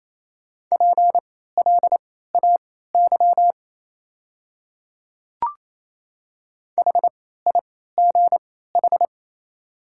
Morse code, heard as a keyed tone, about 28 wpm, spaced as for 16 wpm, 700 Hz. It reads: PLAY HIGH
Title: 3x Character Spacing - Top 200 Words - 28wpm